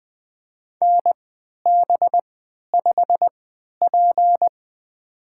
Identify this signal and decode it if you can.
Morse code, heard as a keyed tone, about 20 wpm, spaced as for 11 wpm, 700 Hz. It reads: NB5P
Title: USA Call Signs - 3x Character Spacing - 20wpm